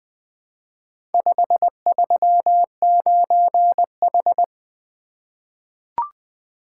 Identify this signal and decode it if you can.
Morse code, heard as a keyed tone, about 20 wpm, 700 Hz. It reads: 539H
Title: Four Letter-Number Combinations - 20wpm